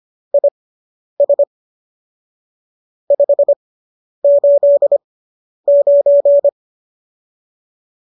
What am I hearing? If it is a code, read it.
Morse code, heard as a keyed tone, about 25 wpm, spaced as for 10 wpm, 570 Hz. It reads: IS 589